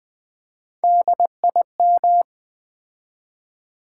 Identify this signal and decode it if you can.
Morse code, heard as a keyed tone, about 20 wpm, 700 Hz. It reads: DIM
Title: Build A Word - 3 Letters At A Time - 9-Letter Words - 20wpm